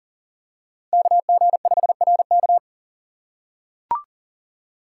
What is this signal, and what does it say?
Morse code, heard as a keyed tone, about 40 wpm, 700 Hz. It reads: KG5RK
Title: USA Call Signs - 5 Characters - 2x Character Spacing - 40wpm